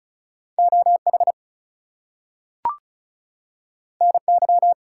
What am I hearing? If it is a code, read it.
Morse code, heard as a keyed tone, about 36 wpm, 700 Hz. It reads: OH NY
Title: US States and Canadian Provinces 35wpm